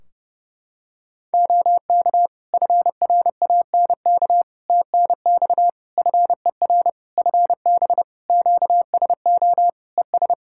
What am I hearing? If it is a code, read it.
Morse code, heard as a keyed tone, about 30 wpm, 700 Hz. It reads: OK FRANK TNX FER FB QSO ES
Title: Practice QSO - Exchange 4 - Ending - 30wpm